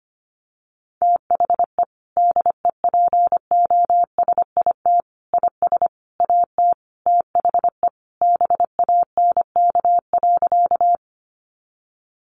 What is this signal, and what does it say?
Morse code, heard as a keyed tone, about 25 wpm, 700 Hz. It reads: THE DEPOSIT IS AT THE BANK.
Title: Sentences from Top 1300 Words 25wpm